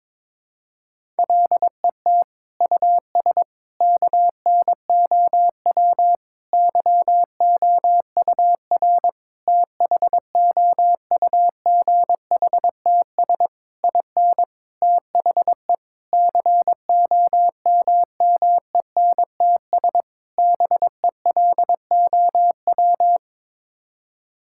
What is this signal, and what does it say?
Morse code, heard as a keyed tone, about 22 wpm, 700 Hz. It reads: LET US KNOW YOUR THOUGHTS IN THE COMMENTS BELOW
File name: Sets of 9 Words - 22wpm